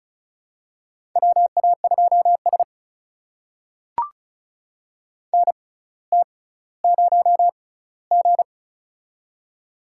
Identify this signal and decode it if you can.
Morse code, heard as a keyed tone, about 36 wpm, spaced as for 18 wpm, 700 Hz. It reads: WA2S NT0G
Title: USA Call Signs - 6x Character Spacing - 35wpm